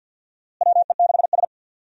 Morse code, heard as a keyed tone, about 51 wpm, 700 Hz. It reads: WE6S
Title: USA Call Signs - 4 Characters - Repeated 3x - 50wpm